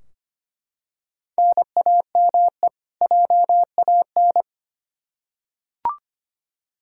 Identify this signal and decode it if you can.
Morse code, heard as a keyed tone, about 25 wpm, 700 Hz. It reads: NAME JAN
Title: Practice QSO - Exchange 1 - Beginning - 25wpm